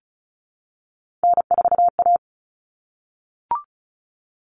Morse code, heard as a keyed tone, about 34 wpm, 700 Hz. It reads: N4A
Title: Common Cut Numbers - 3-Digit 35wpm